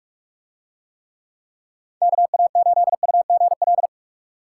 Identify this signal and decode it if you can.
Morse code, heard as a keyed tone, about 45 wpm, 700 Hz. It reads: KA8UGL